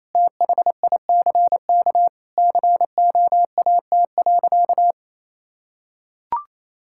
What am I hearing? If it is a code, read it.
Morse code, heard as a keyed tone, about 28 wpm, 700 Hz. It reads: THICK COAT.
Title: Sentences from Top 1600 Words 28wpm